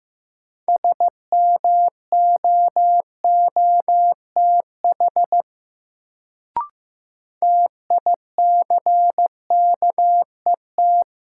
Morse code, heard as a keyed tone, about 15 wpm, 700 Hz. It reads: SMOOTH TICKET
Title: Sentences from Top 1400 Words - New Word Drill Alt - 15wpm